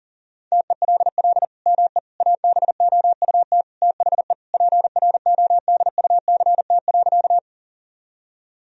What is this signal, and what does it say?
Morse code, heard as a keyed tone, about 40 wpm, 700 Hz. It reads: TELL ME ABOUT THE PRODUCT.
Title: Sentences from Top 1000 Words - Review Part 2 - 40wpm